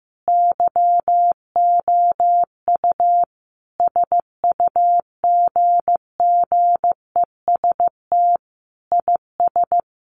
Morse code, heard as a keyed tone, about 15 wpm, 700 Hz. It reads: YOU SUGGEST IS
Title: Sentences from Top 1300 Words 15wpm